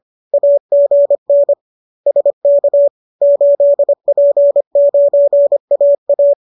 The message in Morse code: AGN SK 8P9AA

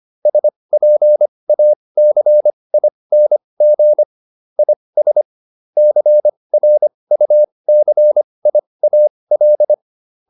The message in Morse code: SPACING IS CRUCIAL

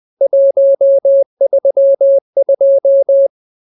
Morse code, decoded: 132

